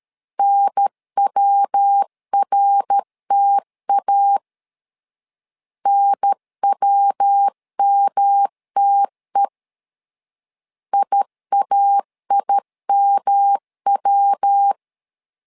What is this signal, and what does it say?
Morse code, heard as a keyed tone, about 13 wpm, 800 Hz. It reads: NWRTA NWMTE IAIMW
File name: LEZIONI / 3 MWRO / 3 MWR 12wpm 1711